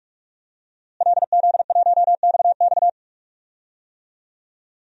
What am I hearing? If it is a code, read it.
Morse code, heard as a keyed tone, about 46 wpm, 700 Hz. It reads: LZ1XX
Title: DXCC-100 Call Signs - 5 Characters - Repeated 2x - 45wpm